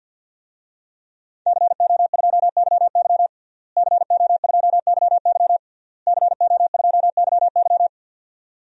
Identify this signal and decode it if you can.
Morse code, heard as a keyed tone, about 52 wpm, 700 Hz. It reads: CO2YY CO2YY CO2YY
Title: DXCC-100 Call Signs - 5 Characters - Repeated 3x - 50wpm